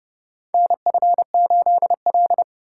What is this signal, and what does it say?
Morse code, heard as a keyed tone, about 30 wpm, 700 Hz. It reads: NF8L
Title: USA Call Signs - 4 Characters - 4x Character Spacing - 30wpm